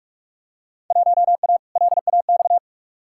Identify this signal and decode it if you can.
Morse code, heard as a keyed tone, about 43 wpm, 700 Hz. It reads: 1A LAX
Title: Field Day Exchanges - 45wpm